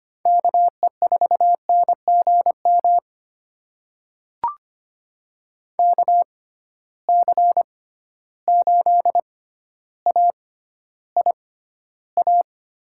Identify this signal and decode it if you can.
Morse code, heard as a keyed tone, about 25 wpm, spaced as for 13 wpm, 700 Hz. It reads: KE4NGM KC8AIA
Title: USA Call Signs - 6 Characters - 6x Character Spacing - 25wpm